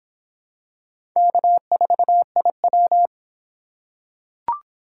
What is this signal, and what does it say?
Morse code, heard as a keyed tone, about 25 wpm, 700 Hz. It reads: K4IW